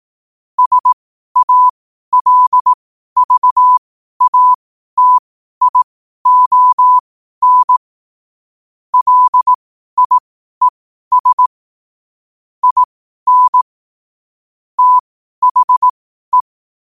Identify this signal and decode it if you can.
Morse code, heard as a keyed tone, about 18 wpm, spaced as for 12 wpm, 1000 Hz. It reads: SALVATION LIES IN THE
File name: qotdfiles / 12wpm